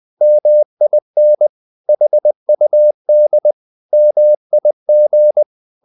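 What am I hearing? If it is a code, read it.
Morse code, heard as a keyed tone, about 20 wpm, 600 Hz. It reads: MIN HUD MIG
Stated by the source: Morse code practice of short Danish words at 20 wpm, from CW med Gnister.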